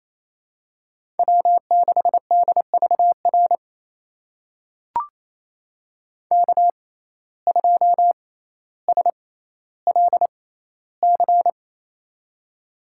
Morse code, heard as a keyed tone, about 28 wpm, spaced as for 14 wpm, 700 Hz. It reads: W6DVR K2SLC
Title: USA Call Signs - 5 Characters - 6x Character Spacing - 28wpm